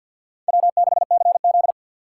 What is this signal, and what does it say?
Morse code, heard as a keyed tone, about 47 wpm, 700 Hz. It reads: W6CZ
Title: USA Call Signs - 4 Characters - 3x Character Spacing - 50wpm